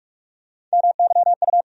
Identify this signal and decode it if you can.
Morse code, heard as a keyed tone, about 45 wpm, 700 Hz. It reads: MYU
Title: USA Call Sign Postfixes - 3 Letters 45wpm